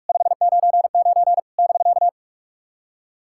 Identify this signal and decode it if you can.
Morse code, heard as a keyed tone, about 46 wpm, 700 Hz. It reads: 599 <BK>
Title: SOTA QSO Practice 45wpm